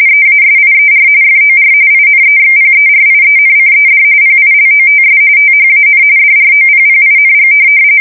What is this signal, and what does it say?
RTTY, about 45.45 baud. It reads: EASIER FOR INEXPERIENCED PILOTS THAN DROPPING
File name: fsk170 / USS-New-Jersey